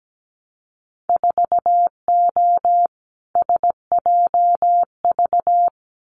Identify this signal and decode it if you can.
Morse code, heard as a keyed tone, about 17 wpm, 700 Hz. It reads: 4O SJV
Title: Winter Field Day Exchanges Encore 17wpm